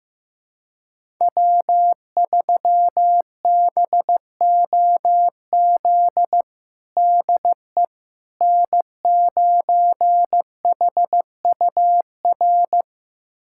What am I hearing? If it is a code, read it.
Morse code, heard as a keyed tone, about 15 wpm, 700 Hz. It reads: W3BOZ DE N9HUR
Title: USA Callsign DE Callsign - 5 Characters - 15wpm